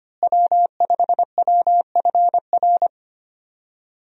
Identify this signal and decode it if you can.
Morse code, heard as a keyed tone, about 25 wpm, 700 Hz. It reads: W5WFR
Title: USA Call Signs - 5 Characters - Repeated 3x - 25wpm